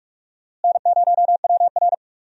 Morse code, heard as a keyed tone, about 44 wpm, 700 Hz. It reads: N0WR
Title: USA Call Signs - 4 Characters - Repeated 2x - 45wpm